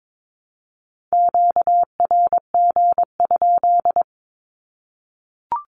Morse code, heard as a keyed tone, about 22 wpm, 700 Hz. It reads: QRG?
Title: Q-Codes 22wpm